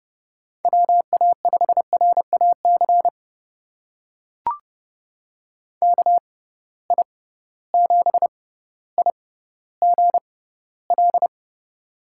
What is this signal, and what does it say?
Morse code, heard as a keyed tone, about 30 wpm, spaced as for 15 wpm, 700 Hz. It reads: WA5RAC KI7IGL
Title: USA Call Signs - 6 Characters - 6x Character Spacing - 30wpm